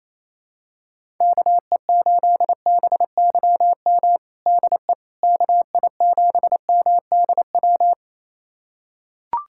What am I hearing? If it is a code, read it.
Morse code, heard as a keyed tone, about 28 wpm, 700 Hz. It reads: KE8BYM DE KI7MDW
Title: USA Callsign DE Callsign - 6 Characters - 28wpm